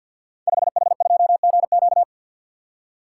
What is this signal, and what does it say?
Morse code, heard as a keyed tone, about 48 wpm, 700 Hz. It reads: 5H2GQ
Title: Code Groups - 5 Letter:Numbers - 50wpm